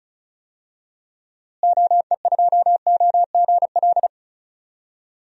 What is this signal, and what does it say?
Morse code, heard as a keyed tone, about 34 wpm, 700 Hz. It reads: OE2OGL
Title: DXCC-100 Call Signs - 6 Characters - Repeated 2x - 35wpm